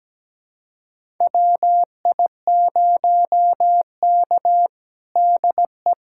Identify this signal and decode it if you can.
Morse code, heard as a keyed tone, about 17 wpm, 700 Hz. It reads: WI0K DE
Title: USA Callsign DE Callsign - 4 Characters - 17wpm